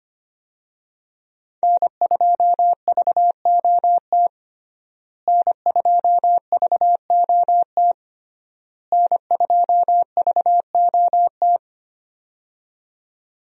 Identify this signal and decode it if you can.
Morse code, heard as a keyed tone, about 25 wpm, 700 Hz. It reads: N2VOT N2VOT N2VOT
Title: USA Call Signs - 5 Characters - Repeated 3x - 25wpm